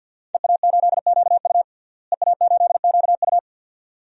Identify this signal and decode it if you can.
Morse code, heard as a keyed tone, about 48 wpm, 700 Hz. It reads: EA8QU EA8QU
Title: DXCC-100 Call Signs - 5 Characters - Repeated 2x - 50wpm